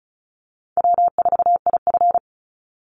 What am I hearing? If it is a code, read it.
Morse code, heard as a keyed tone, about 34 wpm, 700 Hz. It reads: W4IF
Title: Call Signs - 35wpm